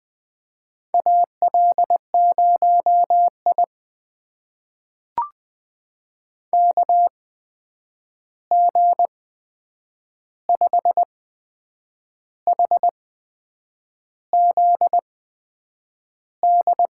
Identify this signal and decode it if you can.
Morse code, heard as a keyed tone, about 20 wpm, spaced as for 7 wpm, 700 Hz. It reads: AL0I KG5HZD
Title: USA Call Signs - 8x Character Spacing - 20wpm